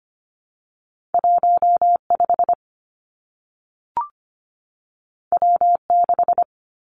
Morse code, heard as a keyed tone, about 25 wpm, 700 Hz. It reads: 1 5 W 6
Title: Two Challenging Letter-Number Combinations 25wpm